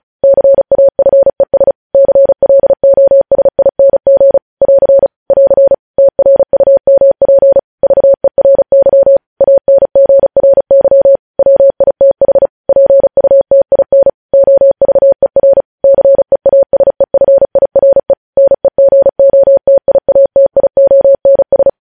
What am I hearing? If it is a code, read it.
Morse code, heard as a keyed tone, about 36 wpm, 550 Hz. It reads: CAFES CLOSING <AR> <AR> TRUMP VERY ANGRY WITH PUTIN OVER CEASEFIRE NEGOTIATIONS
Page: News headlines in Morse code at 35 wpm